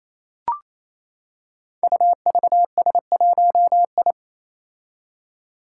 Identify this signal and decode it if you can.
Morse code, heard as a keyed tone, about 29 wpm, 700 Hz. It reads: UVS1I
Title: Five Challenging Letter-Number Combinations 28wpm